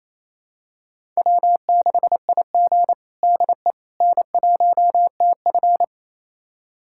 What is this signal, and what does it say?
Morse code, heard as a keyed tone, about 28 wpm, 700 Hz. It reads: W6IG DE N1TF
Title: USA Callsign DE Callsign - 4 Characters - 28wpm